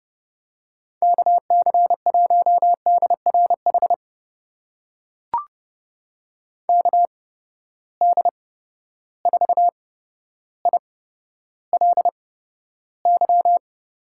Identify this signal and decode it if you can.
Morse code, heard as a keyed tone, about 30 wpm, spaced as for 13 wpm, 700 Hz. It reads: KC1DRH KD4ILY
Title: USA Call Signs - 8x Character Spacing - 30wpm